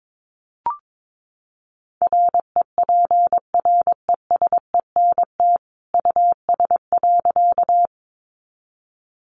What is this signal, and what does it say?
Morse code, heard as a keyed tone, about 22 wpm, 700 Hz. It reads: REPRESENT US.
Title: Sentences from Top 1200 Words 22wpm